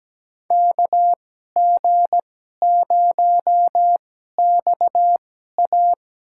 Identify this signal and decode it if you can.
Morse code, heard as a keyed tone, about 17 wpm, spaced as for 12 wpm, 700 Hz. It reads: KG0XA
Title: USA Call Signs - 5 Characters - 2x Character Spacing - 17wpm